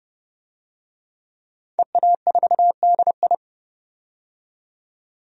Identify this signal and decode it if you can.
Morse code, heard as a keyed tone, about 29 wpm, 690 Hz. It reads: EA4DI
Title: DXCC-100 Call Signs - 5 Characters - 30wpm